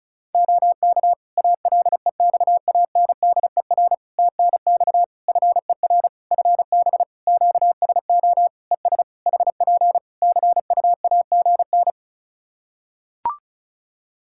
Morse code, heard as a keyed tone, about 36 wpm, 700 Hz. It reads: OK ALEXANDER TNX FER FB QSO ES HP CUAGN
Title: K1USN SST QSO Practice 35wpm